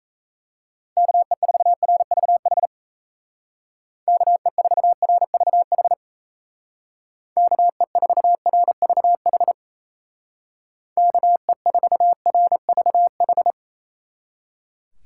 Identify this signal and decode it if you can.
Morse code, heard as a keyed tone, about 37 wpm, 700 Hz. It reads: KE4RVH KE4RVH KE4RVH KE4RVH